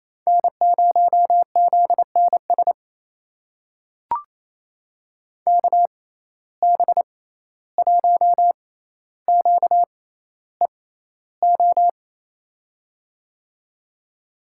Morse code, heard as a keyed tone, about 28 wpm, spaced as for 14 wpm, 700 Hz. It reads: N0ZNS KB1QEO
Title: USA Call Signs - 6x Character Spacing - 28wpm